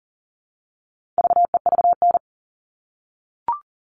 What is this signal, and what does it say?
Morse code, heard as a keyed tone, about 40 wpm, 700 Hz. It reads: VEVN
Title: All Cut Numbers - 4-Digit 40wpm